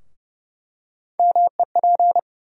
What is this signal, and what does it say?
Morse code, heard as a keyed tone, about 30 wpm, 700 Hz. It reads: MEP
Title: USA Call Sign Postfixes - 3 Letters 30wpm